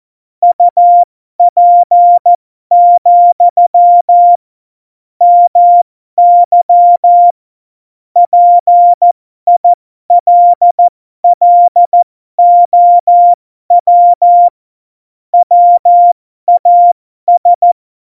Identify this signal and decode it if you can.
Morse code, heard as a keyed tone, about 14 wpm, spaced as for 12 wpm, 700 Hz. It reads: UP, MY PILLOW WAS
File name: Morsekurs / Witze